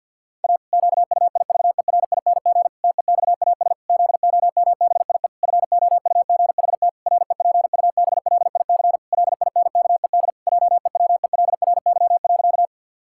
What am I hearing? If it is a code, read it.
Morse code, heard as a keyed tone, about 50 wpm, 700 Hz. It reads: A QUIVERING TEXAS ZOMBIE FOUGHT REPUBLIC LINKED JEWELRY.
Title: Pangram Challenge v2 - 50wpm